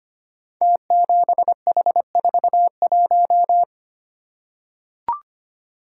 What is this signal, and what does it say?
Morse code, heard as a keyed tone, about 25 wpm, 700 Hz. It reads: T7H41
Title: Code Groups - 5 Letter:Numbers - 25wpm